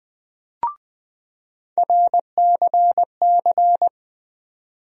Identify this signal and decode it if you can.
Morse code, heard as a keyed tone, about 20 wpm, 700 Hz. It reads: RCC
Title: USA Call Sign Postfixes - 3 Letters 20wpm